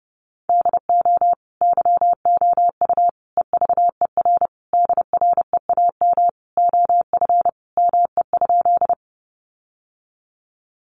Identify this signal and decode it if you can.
Morse code, heard as a keyed tone, about 30 wpm, 700 Hz. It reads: DO YOU EVER DREAM OF ME?
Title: Sentences from Top 1500 Words 30wpm